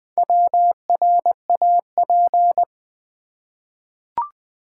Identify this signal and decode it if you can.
Morse code, heard as a keyed tone, about 20 wpm, 700 Hz. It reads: WRAP
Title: Four Letter Words - 20wpm